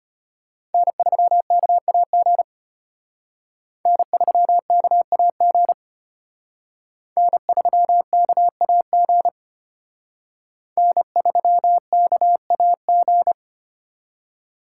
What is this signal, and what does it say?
Morse code, heard as a keyed tone, about 29 wpm, 700 Hz. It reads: N3KAG N3KAG N3KAG N3KAG
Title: Speed Racing - USA 1x3 Call Signs 25wpm